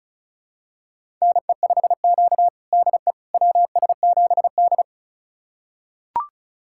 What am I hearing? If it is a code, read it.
Morse code, heard as a keyed tone, about 34 wpm, 700 Hz. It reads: NE5Q DE WS7D
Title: USA Callsign DE Callsign - 4 Characters - 35wpm